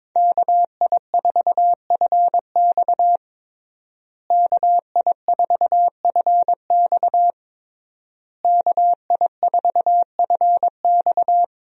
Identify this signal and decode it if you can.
Morse code, heard as a keyed tone, about 21 wpm, 700 Hz. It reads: KI4FX KI4FX KI4FX
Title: USA Call Signs - 5 Characters - Repeated 3x - 22wpm